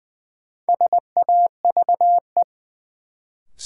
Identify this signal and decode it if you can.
Morse code, heard as a keyed tone, about 20 wpm, 700 Hz. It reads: SAVE
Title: Sentences from Top 1100 Words - New Word Drill - 20wpm